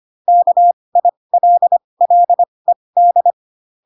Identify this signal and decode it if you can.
Morse code, heard as a keyed tone, about 25 wpm, spaced as for 20 wpm, 700 Hz. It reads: KILLED
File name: Pinocchio / Pinocchio ch7 15 25 ik1hge